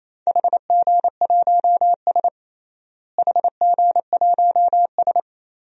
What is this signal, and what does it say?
Morse code, heard as a keyed tone, about 28 wpm, 700 Hz. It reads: HG1S HG1S
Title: DXCC-100 Call Signs - 4 Characters - Repeated 2x - 28wpm